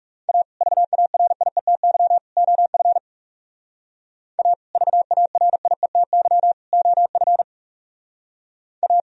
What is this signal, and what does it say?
Morse code, heard as a keyed tone, about 35 wpm, 690 Hz. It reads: A VARIETY OF A VARIETY OF A
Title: Speed-Racing - Sets of 3 Words 30wpm